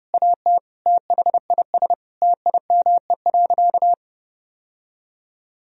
Morse code, heard as a keyed tone, about 30 wpm, 700 Hz. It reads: AT THIS TIME.
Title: Sentences from Top 1000 Words - Review Part 3 - 30wpm